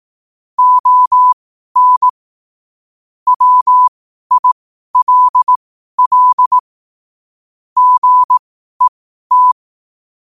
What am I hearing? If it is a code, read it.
Morse code, heard as a keyed tone, about 18 wpm, spaced as for 12 wpm, 1000 Hz. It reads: ON WILL GET
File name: qotdfiles / 12wpm